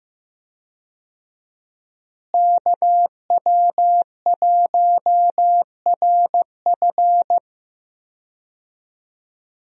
Morse code, heard as a keyed tone, about 15 wpm, 700 Hz. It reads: KW1RF
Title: USA Call Signs - 5 Characters - Repeated 2x - 15wpm